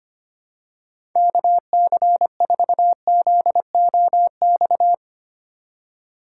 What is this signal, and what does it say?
Morse code, heard as a keyed tone, about 25 wpm, 700 Hz. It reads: KC4ZOX
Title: USA Call Signs - 6 Characters - 2x Character Spacing - 25wpm